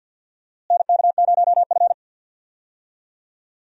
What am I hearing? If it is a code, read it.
Morse code, heard as a keyed tone, about 46 wpm, 700 Hz. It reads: NK0F
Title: USA Call Signs - 4 Characters - Repeated 3x - 50wpm